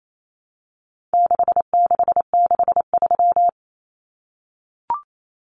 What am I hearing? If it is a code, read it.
Morse code, heard as a keyed tone, about 28 wpm, 700 Hz. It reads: 6663
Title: Numbers - 4-Digit 28wpm